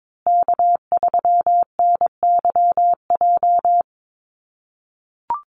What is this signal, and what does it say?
Morse code, heard as a keyed tone, about 22 wpm, 700 Hz. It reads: K3NYJ
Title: USA Call Signs Repeated 3x 22wpm